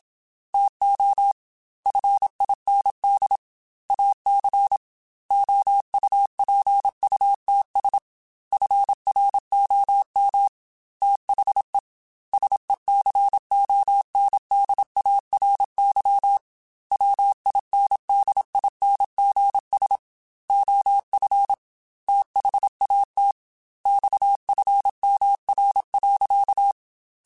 Morse code, heard as a keyed tone, about 27 wpm, 800 Hz. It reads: TO FIND AC OUPUTS FROM THE SECONDARY WINDINGS OF THAT XFMR.
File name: Technical Subjects / Technical Subjects-mp3s / Sw Pwr Sup @25